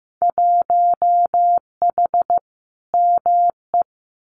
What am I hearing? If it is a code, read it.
Morse code, heard as a keyed tone, about 15 wpm, 700 Hz. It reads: 1H ME
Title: Winter Field Day Exchanges 15wpm